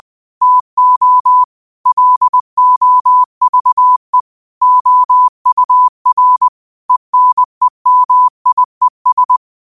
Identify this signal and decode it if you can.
Morse code, heard as a keyed tone, about 20 wpm, 1000 Hz. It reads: TO LOVE OUR ENEMIES
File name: qotdfiles / 20wpm